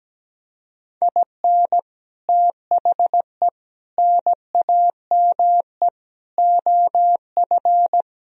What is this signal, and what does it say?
Morse code, heard as a keyed tone, about 17 wpm, 700 Hz. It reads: IN THE NAME OF
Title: Sentences from Top 1000 Words - Review Part 4 - 17wpm